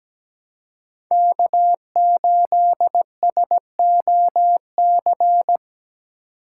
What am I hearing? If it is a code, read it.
Morse code, heard as a keyed tone, about 20 wpm, 700 Hz. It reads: K8SOC
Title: Speed Racing - USA Call Signs 17wpm